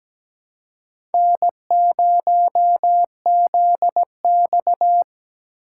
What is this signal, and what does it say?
Morse code, heard as a keyed tone, about 17 wpm, 700 Hz. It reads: N0ZX
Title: USA Call Signs - 4 Characters - 2x Character Spacing - 17wpm